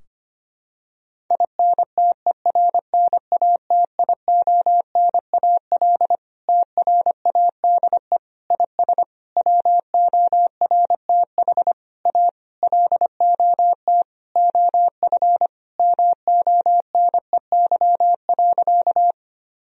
Morse code, heard as a keyed tone, about 25 wpm, 700 Hz. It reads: INTERNATIONAL TRADE IS WORTH A LOT OF MONEY.
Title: Sentences from Top 1000 Words - Review Part 4 - 25wpm